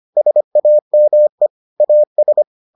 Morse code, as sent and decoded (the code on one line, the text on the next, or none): ... .- -- . / .- ...
SAME AS